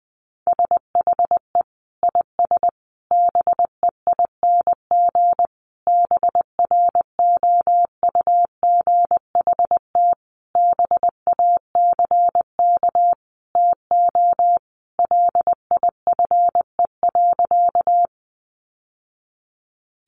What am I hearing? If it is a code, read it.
Morse code, heard as a keyed tone, about 19 wpm, 700 Hz. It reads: SHE IS BEING BROUGHT BACK TO LIFE.
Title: Sentences from Top 600 Words 20wpm